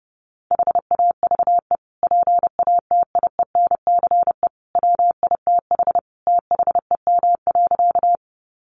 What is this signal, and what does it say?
Morse code, heard as a keyed tone, about 30 wpm, 700 Hz. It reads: HAVE PATIENCE WITH THEM.
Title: Sentences from Top 1300 Words 30wpm